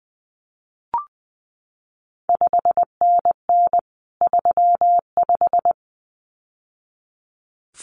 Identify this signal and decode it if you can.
Morse code, heard as a keyed tone, about 20 wpm, 700 Hz. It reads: 5NN 35
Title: CQ WW DX Contest Exchange Encore - 20wpm